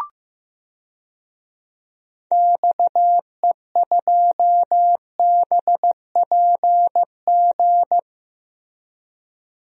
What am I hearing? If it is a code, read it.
Morse code, heard as a keyed tone, about 15 wpm, 700 Hz. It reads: XE2BPG
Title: DXCC-100 Call Signs - 6 Characters - Repeated 3x - 15wpm